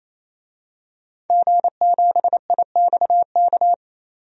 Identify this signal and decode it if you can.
Morse code, heard as a keyed tone, about 28 wpm, 700 Hz. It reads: G7IXK
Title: DXCC-100 Call Signs - 5 Characters - 28wpm